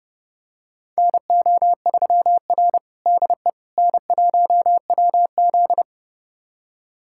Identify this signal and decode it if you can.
Morse code, heard as a keyed tone, about 32 wpm, 700 Hz. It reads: NO3R DE N1WZ
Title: USA Callsign DE Callsign - 4 Characters - 30wpm